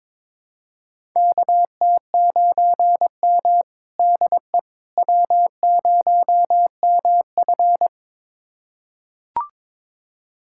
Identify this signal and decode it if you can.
Morse code, heard as a keyed tone, about 22 wpm, 700 Hz. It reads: KT9M DE W0MF
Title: USA Callsign DE Callsign - 4 Characters - 22wpm